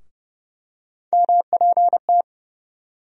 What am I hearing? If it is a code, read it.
Morse code, heard as a keyed tone, about 32 wpm, 700 Hz. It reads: MPT
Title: USA Call Sign Postfixes - 3 Letters 30wpm